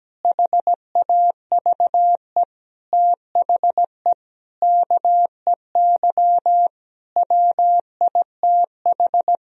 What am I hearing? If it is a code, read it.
Morse code, heard as a keyed tone, about 17 wpm, 700 Hz. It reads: HAVE THE KEY WITH
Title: Sentences from Top 1000 Words - Review Part 4 - 17wpm